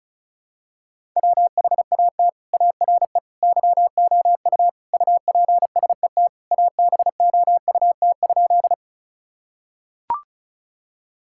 Morse code, heard as a keyed tone, about 34 wpm, 700 Hz. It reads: WHAT ARE YOU UPSET ABOUT?
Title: Sentences from Top 1000 Words - Review Part 5 - 35wpm